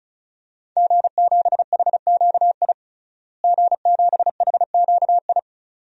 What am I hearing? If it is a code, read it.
Morse code, heard as a keyed tone, about 36 wpm, 700 Hz. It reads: G7HQI G7HQI
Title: DXCC-100 Call Signs - 5 Characters - Repeated 2x - 35wpm